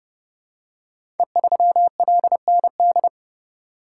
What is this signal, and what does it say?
Morse code, heard as a keyed tone, about 30 wpm, 700 Hz. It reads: E3LND
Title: Code Groups - 5 Letter:Numbers - 30wpm